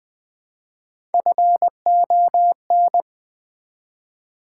Morse code, heard as a keyed tone, about 20 wpm, 700 Hz. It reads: FON
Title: Build A Word - 1 Letter At A Time - 4-Letter Words - 20wpm